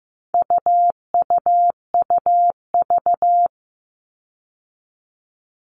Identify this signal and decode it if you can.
Morse code, heard as a keyed tone, about 15 wpm, 700 Hz. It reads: UUUV
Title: All Cut Numbers - 4-Digit 15wpm